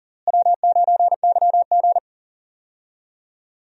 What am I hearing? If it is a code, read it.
Morse code, heard as a keyed tone, about 39 wpm, 700 Hz. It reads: W9YG
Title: USA Call Signs - 4 Characters - Repeated 2x - 40wpm